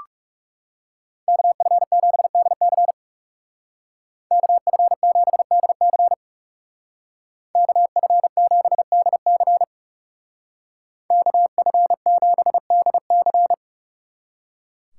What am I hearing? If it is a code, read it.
Morse code, heard as a keyed tone, about 39 wpm, 700 Hz. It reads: KF7DC KF7DC KF7DC KF7DC